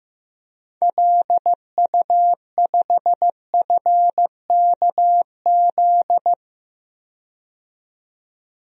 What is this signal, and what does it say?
Morse code, heard as a keyed tone, about 15 wpm, 700 Hz. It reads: LU5FKZ
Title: DXCC-100 Call Signs - 6 Characters - 15wpm